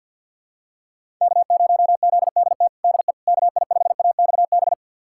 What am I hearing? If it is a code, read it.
Morse code, heard as a keyed tone, about 51 wpm, 700 Hz. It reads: K0ZDT DE KI5AXB
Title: USA Callsign DE Callsign 50wpm